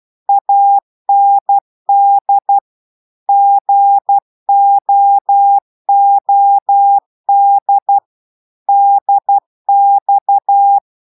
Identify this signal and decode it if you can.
Morse code, heard as a keyed tone, about 12 wpm, 800 Hz. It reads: AND GOOD DX